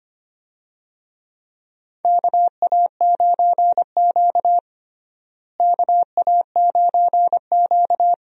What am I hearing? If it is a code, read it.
Morse code, heard as a keyed tone, about 25 wpm, 700 Hz. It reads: KA9Q KA9Q
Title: USA Call Signs - 4 Characters - Repeated 2x - 25wpm